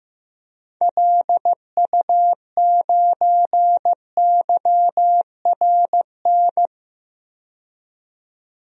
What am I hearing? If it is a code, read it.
Morse code, heard as a keyed tone, about 15 wpm, 700 Hz. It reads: LU9YRN
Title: DXCC-100 Call Signs - 6 Characters - 15wpm